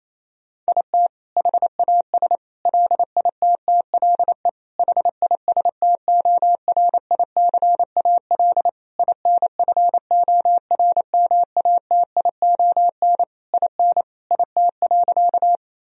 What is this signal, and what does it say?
Morse code, heard as a keyed tone, about 28 wpm, 700 Hz. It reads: IT HAS LITTLE HISTORICAL INFORMATION IN IT.
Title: Sentences from Top 1000 Words - Review Part 4 - 28wpm